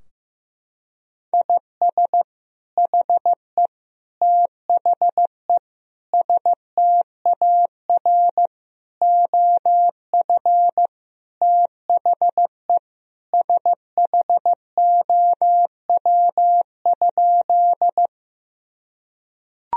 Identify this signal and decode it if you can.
Morse code, heard as a keyed tone, about 15 wpm, 700 Hz. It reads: IS HE THE STAR OF THE SHOW?
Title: Sentences from Top 1000 Words - Review Part 2 - 15wpm